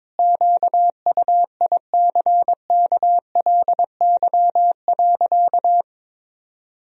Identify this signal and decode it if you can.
Morse code, heard as a keyed tone, about 22 wpm, 700 Hz. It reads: QUICKLY.
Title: Sentences from Top 1000 Words - Review Part 1 - 22wpm